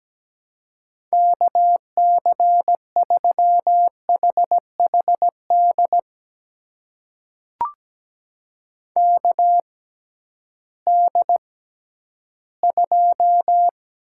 Morse code, heard as a keyed tone, about 17 wpm, spaced as for 11 wpm, 700 Hz. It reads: KC3HHD KD2
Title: USA Call Signs - 6 Characters - 6x Character Spacing - 17wpm